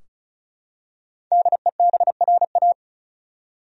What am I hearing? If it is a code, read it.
Morse code, heard as a keyed tone, about 34 wpm, 700 Hz. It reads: DEBRA
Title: USA Common Names - 35wpm